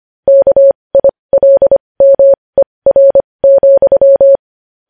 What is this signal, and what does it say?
Morse code, heard as a keyed tone, about 25 wpm, spaced as for 20 wpm, 550 Hz. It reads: KILMER,
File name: podcast / News headlines-1x-25x20